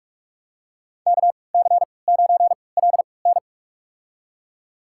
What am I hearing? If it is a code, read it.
Morse code, heard as a keyed tone, about 46 wpm, spaced as for 25 wpm, 700 Hz. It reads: KC9LN
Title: USA Call Signs - 3x Character Spacing - 45wpm